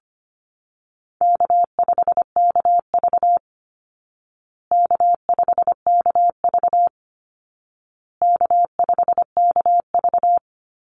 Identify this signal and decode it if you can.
Morse code, heard as a keyed tone, about 25 wpm, 700 Hz. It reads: K5KV K5KV K5KV
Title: Call Signs Repeated 2x - 25wpm